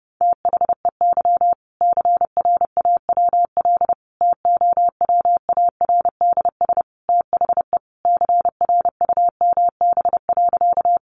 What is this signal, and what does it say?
Morse code, heard as a keyed tone, about 30 wpm, 700 Hz. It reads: THEY CRAWL TOWARDS THE CRUMB.
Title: Easy Phrases 30wpm